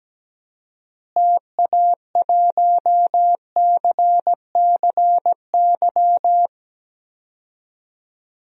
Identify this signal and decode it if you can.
Morse code, heard as a keyed tone, about 17 wpm, 700 Hz. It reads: TA1CCY
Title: DXCC-100 Call Signs - 6 Characters - Repeated 2x - 17wpm